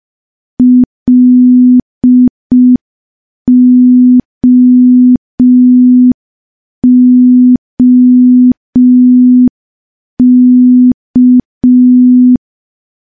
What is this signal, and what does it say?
Morse code, heard as a keyed tone, about 5 wpm, 250 Hz. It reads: LOOK
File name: morse / morse-training-05WPM